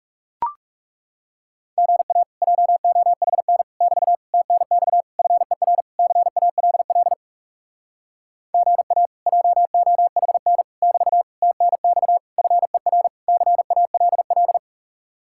Morse code, heard as a keyed tone, about 37 wpm, 700 Hz. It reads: GA JOHN <BT> TNX FER CALL GA JOHN <BT> TNX FER CALL
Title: Speed-Racing Common QSO Phrases Encore - 30wpm